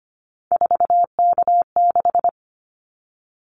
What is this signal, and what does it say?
Morse code, heard as a keyed tone, about 25 wpm, 700 Hz. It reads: 4K6
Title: Callsign Prefixes for Top 100 DXCC - 25wpm